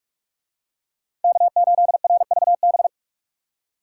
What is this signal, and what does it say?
Morse code, heard as a keyed tone, about 44 wpm, 700 Hz. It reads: K8RVB